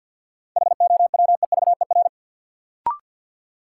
Morse code, heard as a keyed tone, about 50 wpm, 700 Hz. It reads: HOWEVER
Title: Sentences from Top 600 Words - New Word Drill - 50wpm